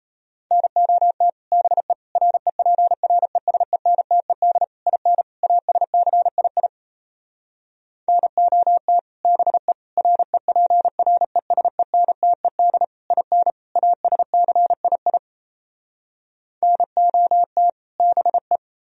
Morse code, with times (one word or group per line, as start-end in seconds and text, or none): 0.5-1.3 NOT
1.5-1.9 BE
2.2-4.6 REPRESENTED
4.9-5.2 IN
5.4-6.7 ASCII
8.1-9.0 NOT
9.2-9.7 BE
10.0-12.8 REPRESENTED
13.1-13.5 IN
13.8-15.2 ASCII
16.6-17.7 NOT
18.0-18.6 BE